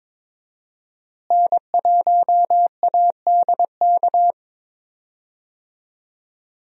Morse code, heard as a keyed tone, about 22 wpm, 700 Hz. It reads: N1ADK
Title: USA Call Signs - 5 Characters - Repeated 3x - 22wpm